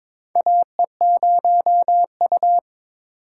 Morse code, heard as a keyed tone, about 22 wpm, 700 Hz. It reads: AE0U
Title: USA Call Signs - 4 Characters - 4x Character Spacing - 22wpm